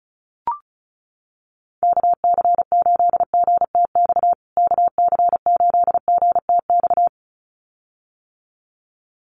Morse code, heard as a keyed tone, about 34 wpm, 700 Hz. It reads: KC8GTX KC8GTX
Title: USA Call Signs Repeated 2x 35wpm